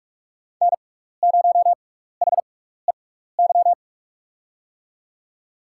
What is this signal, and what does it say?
Morse code, heard as a keyed tone, about 44 wpm, spaced as for 15 wpm, 700 Hz. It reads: N0HEY
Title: USA Call Signs - 5 Characters - 6x Character Spacing - 45wpm